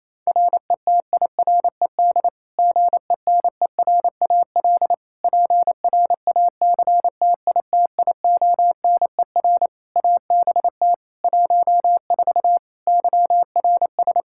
Morse code, heard as a keyed tone, about 27 wpm, 700 Hz. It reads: RETIRED GENERAL PRACTITIONER ABT 14 YRS
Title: Practice QSO - Exchange 3 - More Info - 28wpm